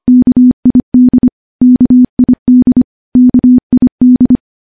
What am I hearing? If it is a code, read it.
Morse code, heard as a keyed tone, about 25 wpm, 250 Hz. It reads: KID KID KID
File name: morse-training-25WPM